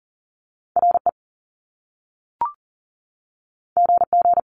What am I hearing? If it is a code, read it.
Morse code, heard as a keyed tone, about 42 wpm, 700 Hz. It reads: RE GG
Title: Top Two Letter Combinations in Words 40wpm